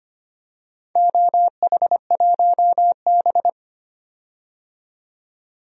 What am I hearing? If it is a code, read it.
Morse code, heard as a keyed tone, about 25 wpm, 700 Hz. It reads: OH1B
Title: DXCC-100 Call Signs - 4 Characters - 25wpm